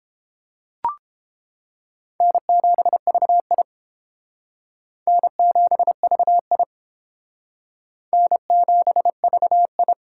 Morse code, heard as a keyed tone, about 32 wpm, 700 Hz. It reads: N7VI N7VI N7VI